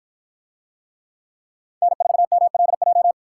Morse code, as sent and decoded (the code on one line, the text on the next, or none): -. ....- -- .-.. .---
N4MLJ